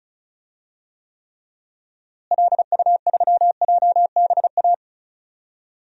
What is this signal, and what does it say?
Morse code, heard as a keyed tone, about 34 wpm, 700 Hz. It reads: LU3JBA
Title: DXCC-100 Call Signs - 6 Characters - Repeated 2x - 35wpm